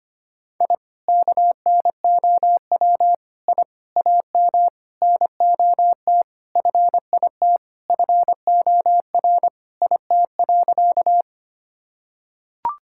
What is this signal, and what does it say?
Morse code, heard as a keyed tone, about 25 wpm, 700 Hz. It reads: I KNOW I AM NOT FIT FOR IT.
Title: Sentences from Top 1000 Words - Review Part 5 - 25wpm